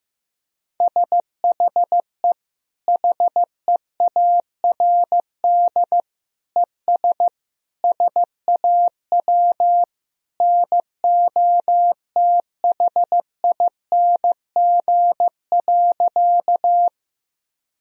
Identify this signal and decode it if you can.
Morse code, heard as a keyed tone, about 15 wpm, 700 Hz. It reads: SHE HEARD ES SAW NOTHING.
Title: Sentences from Top 500 Words and Abbreviations 15wpm